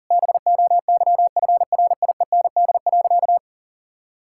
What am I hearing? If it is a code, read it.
Morse code, heard as a keyed tone, about 40 wpm, 700 Hz. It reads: BOYFRIEND.
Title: Sentences from Top 1000 Words - Review Part 5 - 40wpm